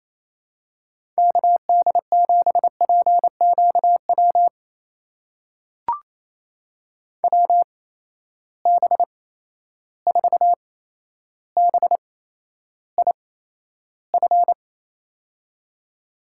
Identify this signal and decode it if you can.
Morse code, heard as a keyed tone, about 28 wpm, spaced as for 12 wpm, 700 Hz. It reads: KD7PQW WB4BIF
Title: USA Call Signs - 6 Characters - 8x Character Spacing - 28wpm